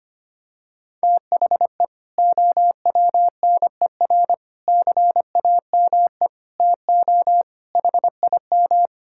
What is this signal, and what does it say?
Morse code, heard as a keyed tone, about 25 wpm, 700 Hz. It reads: THE OWNER CAME TO HIM
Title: Sentences from Top 1600 Words 25wpm